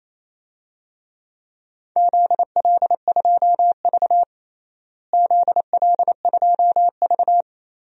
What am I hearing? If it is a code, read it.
Morse code, heard as a keyed tone, about 28 wpm, 700 Hz. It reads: ZL2V ZL2V
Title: DXCC-100 Call Signs - 4 Characters - Repeated 2x - 28wpm